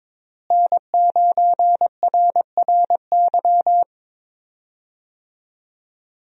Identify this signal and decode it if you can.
Morse code, heard as a keyed tone, about 22 wpm, 700 Hz. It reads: N9RRY
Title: USA Call Signs - 5 Characters - Repeated 2x - 22wpm